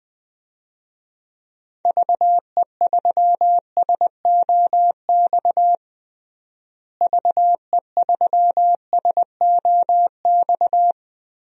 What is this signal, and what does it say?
Morse code, heard as a keyed tone, about 19 wpm, 700 Hz. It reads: VE3SOX VE3SOX
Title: DXCC-100 Call Signs - 6 Characters - Repeated 3x - 20wpm